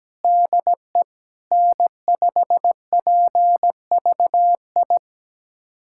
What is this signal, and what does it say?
Morse code, heard as a keyed tone, about 17 wpm, 700 Hz. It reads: DE N5PVI
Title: USA Callsign DE Callsign 17wpm